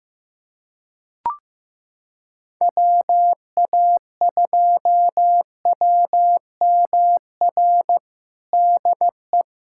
Morse code, heard as a keyed tone, about 15 wpm, 700 Hz. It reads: WA2WMR DE
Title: USA Callsign DE Callsign - 6 Characters - 15wpm